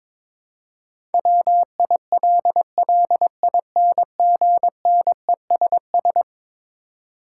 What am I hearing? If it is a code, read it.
Morse code, heard as a keyed tone, about 22 wpm, 700 Hz. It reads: WILLINGNESS
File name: practice / 22wpm / Eleven Letter Words - 22wpm